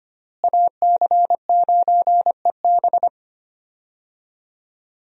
Morse code, heard as a keyed tone, about 25 wpm, 700 Hz. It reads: AC9EB
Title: USA Call Signs - 5 Characters - Repeated 3x - 25wpm